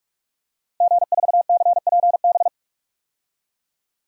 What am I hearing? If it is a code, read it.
Morse code, heard as a keyed tone, about 44 wpm, 700 Hz. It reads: G4CPB
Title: DXCC-100 Call Signs - 5 Characters - 45wpm